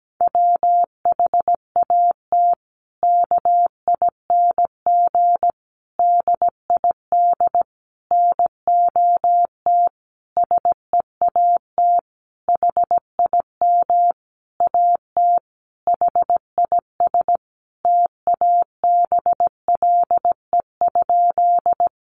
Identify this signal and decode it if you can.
Morse code, heard as a keyed tone, about 17 wpm, 700 Hz. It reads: WHAT KING DID NOT SEAT HIM AT HIS TABLE?
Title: Sentences from Top 1500 Words 17wpm